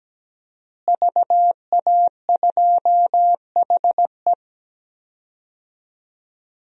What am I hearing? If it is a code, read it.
Morse code, heard as a keyed tone, about 17 wpm, 700 Hz. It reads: VA2HE
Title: DXCC-100 Call Signs - 5 Characters - Repeated 2x - 17wpm